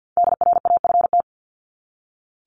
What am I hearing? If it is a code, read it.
Morse code, heard as a keyed tone, about 52 wpm, 700 Hz. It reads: DRAFT
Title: Top 1751-2000 Words 50wpm